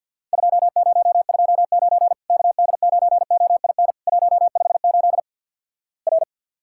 Morse code, heard as a keyed tone, about 53 wpm, 700 Hz. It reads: 2029 KD9OIN 158 R
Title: Worked All Europe DX Contest - QTC Exchange - 50wpm